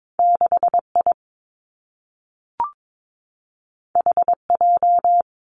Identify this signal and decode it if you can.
Morse code, heard as a keyed tone, about 22 wpm, 700 Hz. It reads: 6 I H J
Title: Two Challenging Letter-Number Combinations 22wpm